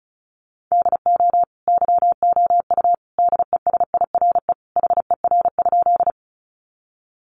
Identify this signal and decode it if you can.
Morse code, heard as a keyed tone, about 34 wpm, 700 Hz. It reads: DO YOU DESIRE HER?
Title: Sentences from Top 1100 Words 35wpm